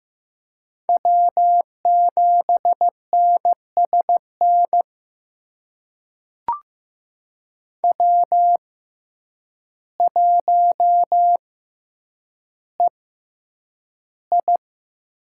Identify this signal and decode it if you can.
Morse code, heard as a keyed tone, about 15 wpm, spaced as for 8 wpm, 700 Hz. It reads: W7NSN W1EI
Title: USA Call Signs - 5 Characters - 6x Character Spacing - 15wpm